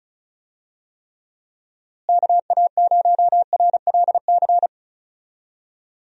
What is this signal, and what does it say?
Morse code, heard as a keyed tone, about 36 wpm, 700 Hz. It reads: KA0RLC
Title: USA Call Signs - 6 Characters - Repeated 3x - 35wpm